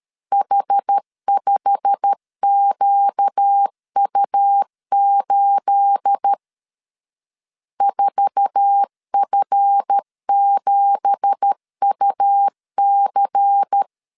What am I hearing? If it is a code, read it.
Morse code, heard as a keyed tone, about 13 wpm, 800 Hz. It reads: H5QU8 4F7UC
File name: CW / DET / 1 9 1500C S B 12wpm 3819